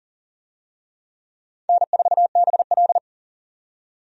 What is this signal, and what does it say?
Morse code, heard as a keyed tone, about 40 wpm, 700 Hz. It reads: N4BL
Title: USA Call Signs - 4 Characters - 40wpm